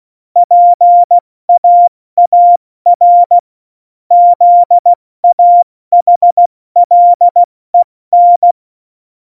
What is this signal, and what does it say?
Morse code, heard as a keyed tone, about 16 wpm, 700 Hz. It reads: PAAR ZAHLEN
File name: Morsekurs / Witze / Witz 26. Januar 2021, 70 F80